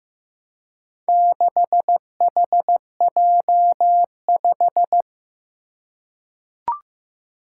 Morse code, heard as a keyed tone, about 15 wpm, 700 Hz. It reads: 6HJ5
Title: Four Challenging Letter-Number Combinations 15wpm